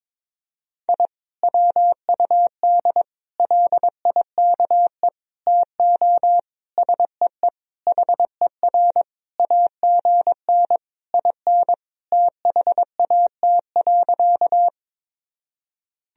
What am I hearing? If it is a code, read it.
Morse code, heard as a keyed tone, about 22 wpm, 700 Hz. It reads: I WUD LIKE TO SEE HER AGN IN THAT.
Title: Sentences from Top 200 Words and Abbreviations 22wpm